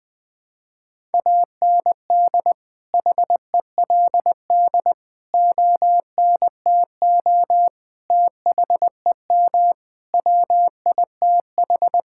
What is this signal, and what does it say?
Morse code, heard as a keyed tone, about 21 wpm, 700 Hz. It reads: AND HELD ONTO THEM WITH
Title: Sentences with 9-Word Phrases - Spoken Beginning - 20wpm